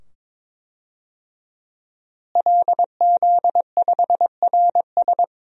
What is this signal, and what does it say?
Morse code, heard as a keyed tone, about 22 wpm, 700 Hz. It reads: LZ5RS
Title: DXCC-100 Call Signs - 5 Characters - Repeated 2x - 22wpm